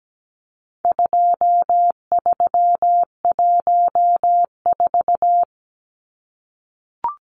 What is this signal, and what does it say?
Morse code, heard as a keyed tone, about 17 wpm, 700 Hz. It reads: 2314
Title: Numbers - 4-Digit 17wpm